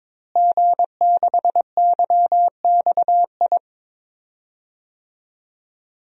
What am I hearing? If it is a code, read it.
Morse code, heard as a keyed tone, about 22 wpm, 700 Hz. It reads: G6YXI
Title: DXCC-100 Call Signs - 5 Characters - Repeated 3x - 22wpm